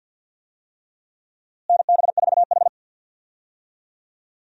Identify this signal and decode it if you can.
Morse code, heard as a keyed tone, about 51 wpm, 700 Hz. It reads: NB4H
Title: USA Call Signs - 4 Characters - Repeated 3x - 50wpm